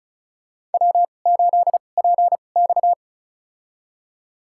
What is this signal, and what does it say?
Morse code, heard as a keyed tone, about 34 wpm, spaced as for 25 wpm, 700 Hz. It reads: W8PX
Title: USA Call Signs - 4 Characters - 2x Character Spacing - 35wpm